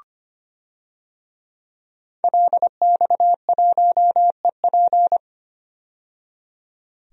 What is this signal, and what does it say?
Morse code, heard as a keyed tone, about 25 wpm, 700 Hz. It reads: LX1EP
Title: DXCC-100 Call Signs - 5 Characters - 25wpm